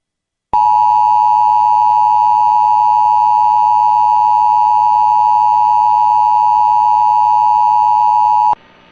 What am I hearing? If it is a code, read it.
Emergency Alert System (SAME) tones (the 853 and 960 Hz attention tone). It is heard from 0.5 to 8.6 s.